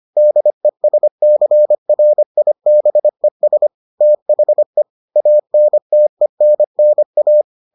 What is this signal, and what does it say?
Morse code, heard as a keyed tone, about 25 wpm, 600 Hz. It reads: DESCRIBES THE ANTENNA